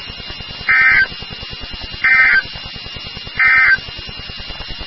Emergency Alert System (SAME) tones, heard as harsh buzzing bursts. It is heard from 0.6 to 3.8 s.
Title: NOAA Weather Radio WXK36